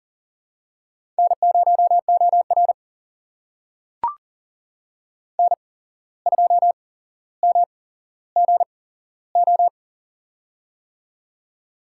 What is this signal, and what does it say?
Morse code, heard as a keyed tone, about 42 wpm, spaced as for 16 wpm, 700 Hz. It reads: N0OR N2MGO
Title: USA Call Signs - 8x Character Spacing - 40wpm